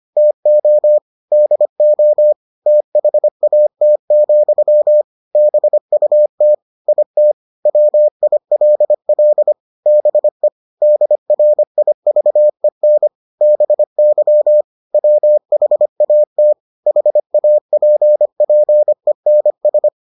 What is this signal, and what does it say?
Morse code, heard as a keyed tone, about 25 wpm, 600 Hz. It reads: TO DO THAT, BUT IT WILL BE DRIVEN BY WHAT HAPPENS